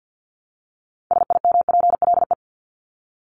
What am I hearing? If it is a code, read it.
Morse code, heard as a keyed tone, about 49 wpm, 700 Hz. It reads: SIMPLE
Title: Top 500 Words Repeated 50wpm